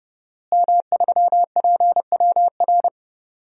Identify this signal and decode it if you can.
Morse code, heard as a keyed tone, about 30 wpm, 700 Hz. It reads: M3PWR
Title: DXCC-100 Call Signs - 5 Characters - Repeated 2x - 30wpm